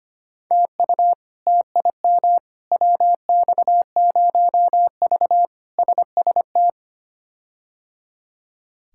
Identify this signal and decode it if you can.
Morse code, heard as a keyed tone, about 25 wpm, 700 Hz. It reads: TU TIM WX0V SST
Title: K1USN SST QSO Practice - 25wpm